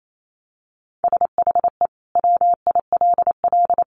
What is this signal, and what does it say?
Morse code, heard as a keyed tone, about 28 wpm, 700 Hz. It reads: SHE WILL
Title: Sentences from Top 1500 Words 28wpm